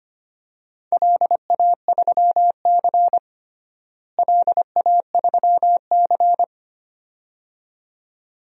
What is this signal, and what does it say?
Morse code, heard as a keyed tone, about 25 wpm, 700 Hz. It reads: LA3C LA3C
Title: DXCC-100 Call Signs - 4 Characters - Repeated 2x - 25wpm